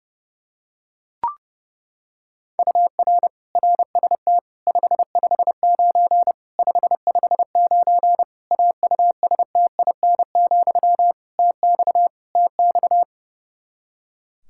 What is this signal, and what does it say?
Morse code, heard as a keyed tone, about 30 wpm, 700 Hz. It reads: UR RST 559 559 AUSTIN, TX TX
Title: K1USN SST QSO Practice 30wpm